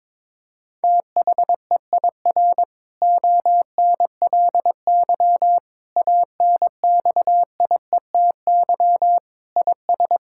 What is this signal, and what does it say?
Morse code, heard as a keyed tone, about 22 wpm, 700 Hz. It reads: THEIR ONLY ANXIETY IS
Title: Sentences from Top 1600 Words 22wpm